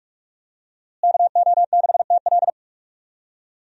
Morse code, heard as a keyed tone, about 42 wpm, 700 Hz. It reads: KO6TL
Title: USA Call Signs - 5 Characters - Repeated 2x - 45wpm